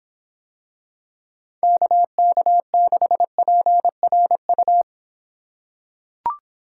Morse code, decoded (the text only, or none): KK6PRU